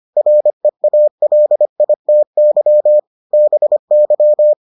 Morse code, heard as a keyed tone, about 25 wpm, 600 Hz. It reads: REALITY BY